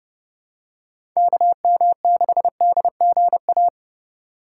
Morse code, heard as a keyed tone, about 32 wpm, 700 Hz. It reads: KM6DGA
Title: USA Call Signs - 6 Characters - 4x Character Spacing - 30wpm